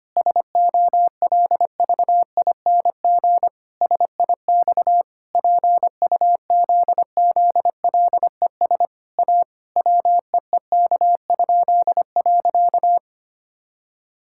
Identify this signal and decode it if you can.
Morse code, heard as a keyed tone, about 25 wpm, 700 Hz. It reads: SOLVING SIX PUZZLES A WEEK?.
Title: Pangram Challenge v2 - 25wpm